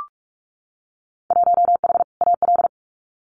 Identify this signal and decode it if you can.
Morse code, heard as a keyed tone, about 46 wpm, 700 Hz. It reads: 1H AL
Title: Winter Field Day Exchanges 45wpm